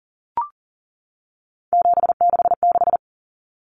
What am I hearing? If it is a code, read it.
Morse code, heard as a keyed tone, about 39 wpm, 700 Hz. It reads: 766
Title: Common Cut Numbers - 3-Digit 40wpm